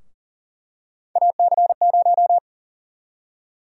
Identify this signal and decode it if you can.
Morse code, heard as a keyed tone, about 40 wpm, 700 Hz. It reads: AC0
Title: USA Call Sign Prefixes - Repeated 2x - 40wpm